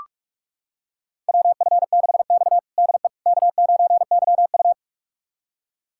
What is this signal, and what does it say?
Morse code, heard as a keyed tone, about 45 wpm, 700 Hz. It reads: WF6X DE K9YU
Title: USA Callsign DE Callsign - 4 Characters - 45wpm